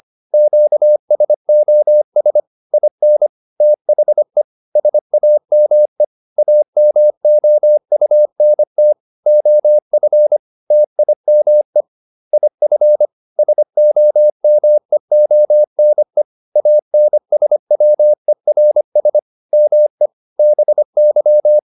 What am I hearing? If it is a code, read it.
Morse code, heard as a keyed tone, about 25 wpm, 600 Hz. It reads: QSOS IN THE SAME AMOUNT OF TIME IF SOMEONE ANSWERS ME BY